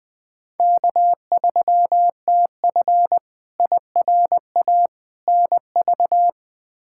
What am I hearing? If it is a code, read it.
Morse code, heard as a keyed tone, about 21 wpm, 700 Hz. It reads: K3TF IRA NV
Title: K1USN SST QSO Practice - 20wpm